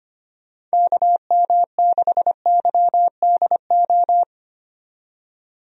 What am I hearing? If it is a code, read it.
Morse code, heard as a keyed tone, about 25 wpm, 700 Hz. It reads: KM6YDO
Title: USA Call Signs - 6 Characters - 25wpm